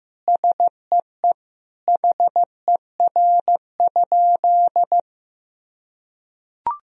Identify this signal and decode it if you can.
Morse code, heard as a keyed tone, about 15 wpm, 700 Hz. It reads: SEE HER?
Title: Sentences from Top 100 Words and Abbreviations 15wpm